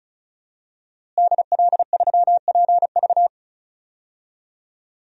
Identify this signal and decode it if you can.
Morse code, heard as a keyed tone, about 34 wpm, 700 Hz. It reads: DL3PV
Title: DXCC-100 Call Signs - 5 Characters - Repeated 2x - 35wpm